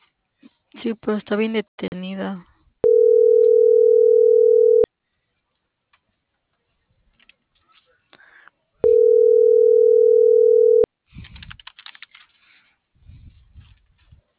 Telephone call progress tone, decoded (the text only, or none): ringback tone